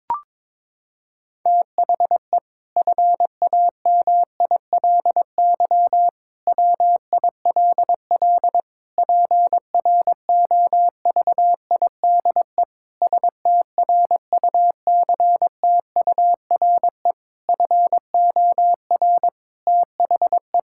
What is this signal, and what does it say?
Morse code, heard as a keyed tone, about 21 wpm, 700 Hz. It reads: THE FAMILY WILL PROVIDE STRUCTURE FOR THE
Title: Sentences from Top 1000 Words - Review Part 6 - 22wpm